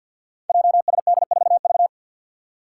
Morse code, heard as a keyed tone, about 52 wpm, 700 Hz. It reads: JSD4V
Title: Five Challenging Letter-Number Combinations 50wpm